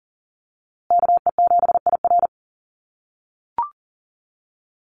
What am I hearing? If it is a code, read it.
Morse code, heard as a keyed tone, about 39 wpm, 700 Hz. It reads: KE7IR
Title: USA Call Signs Repeated 2x 40wpm